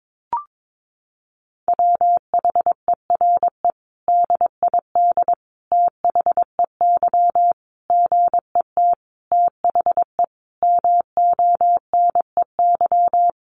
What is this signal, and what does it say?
Morse code, heard as a keyed tone, about 22 wpm, 700 Hz. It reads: WHERE DID THEY GET THE MONEY
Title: Sentences from Top 400 Words 22wpm